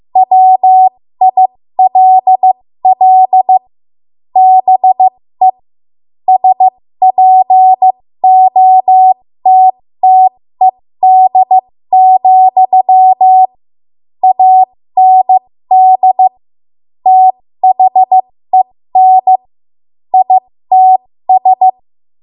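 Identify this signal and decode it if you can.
Morse code, heard as a keyed tone, about 15 wpm, spaced as for 13 wpm, 750 Hz. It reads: WILL BE SPOTTED, AND THEN ITS
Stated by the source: 13 wpm Morse code practice archive transmission.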